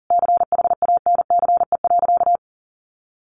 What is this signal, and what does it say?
Morse code, heard as a keyed tone, about 40 wpm, 700 Hz. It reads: CHANCE.
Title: Sentences from Top 1400 Words 40wpm